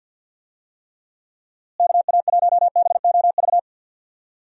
Morse code, heard as a keyed tone, about 51 wpm, 700 Hz. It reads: KA1BOV
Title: USA Call Signs - 6 Characters - Repeated 3x - 50wpm